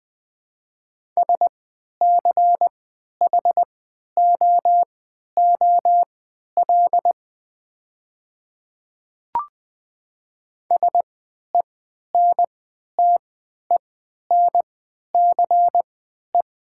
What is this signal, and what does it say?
Morse code, heard as a keyed tone, about 20 wpm, spaced as for 11 wpm, 690 Hz. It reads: SCHOOL SENTENCE
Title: 3x Character Spacing - Top 300 Words - 20wpm